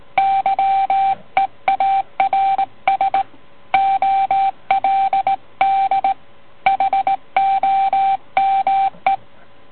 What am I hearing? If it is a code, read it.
Morse code, heard as a keyed tone, about 19 wpm, 780 Hz. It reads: YEARS OLD HOME